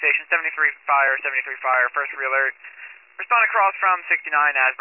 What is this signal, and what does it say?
Emergency Alert System (SAME) tones, heard as harsh buzzing bursts, with speech. It is heard from 0.4 to 4.4 s.